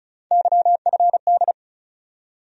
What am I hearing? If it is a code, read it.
Morse code, heard as a keyed tone, about 36 wpm, 700 Hz. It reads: YFD
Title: USA Call Sign Postfixes - 3 Letters 35wpm